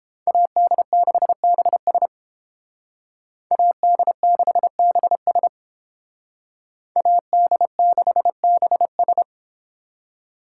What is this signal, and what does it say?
Morse code, heard as a keyed tone, about 28 wpm, 700 Hz. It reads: AD6BS AD6BS AD6BS